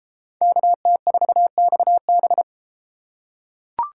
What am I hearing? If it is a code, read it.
Morse code, heard as a keyed tone, about 31 wpm, 700 Hz. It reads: KT4XB